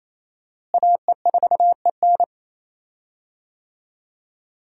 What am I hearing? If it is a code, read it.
Morse code, heard as a keyed tone, about 28 wpm, 700 Hz. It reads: AE4EN
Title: USA Call Signs - 5 Characters - Repeated 3x - 28wpm